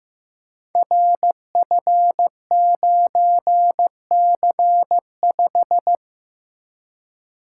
Morse code, heard as a keyed tone, about 15 wpm, 700 Hz. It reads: RF9C5
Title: Code Groups - 5 Letter:Numbers - 15wpm